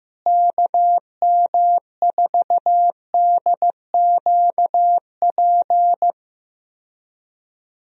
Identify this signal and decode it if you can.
Morse code, heard as a keyed tone, about 15 wpm, 700 Hz. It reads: KM4DQP
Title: USA Call Signs - 6 Characters - Repeated 3x - 15wpm